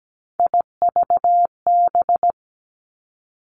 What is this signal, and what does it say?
Morse code, heard as a keyed tone, about 17 wpm, 700 Hz. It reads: IVB
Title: Three Challenging Letter-Number Combinations 17wpm